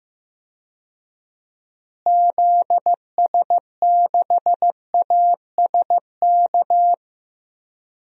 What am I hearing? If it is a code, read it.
Morse code, heard as a keyed tone, about 15 wpm, 700 Hz. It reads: ZS6ASK
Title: DXCC-100 Call Signs - 6 Characters - Repeated 3x - 15wpm